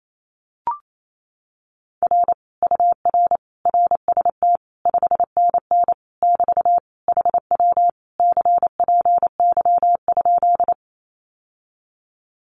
Morse code, heard as a keyed tone, about 28 wpm, 700 Hz. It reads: R UR RST 5NN <BT> HW CPY?
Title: Common QSO Phrases Encore 28wpm